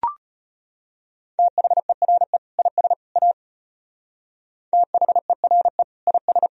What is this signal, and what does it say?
Morse code, heard as a keyed tone, about 32 wpm, 700 Hz. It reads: THERE IS A THERE IS